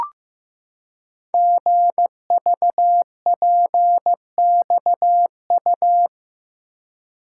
Morse code, heard as a keyed tone, about 15 wpm, 700 Hz. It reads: GVPXU
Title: Code Groups - 5 Letters - 15wpm